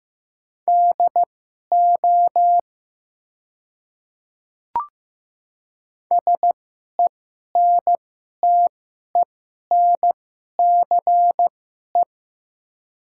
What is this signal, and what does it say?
Morse code, heard as a keyed tone, about 15 wpm, spaced as for 11 wpm, 700 Hz. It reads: DO SENTENCE
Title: 2x Character Spacing - Top 200 Words - 15wpm